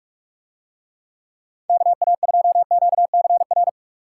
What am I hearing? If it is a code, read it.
Morse code, heard as a keyed tone, about 49 wpm, 700 Hz. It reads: KA2QCR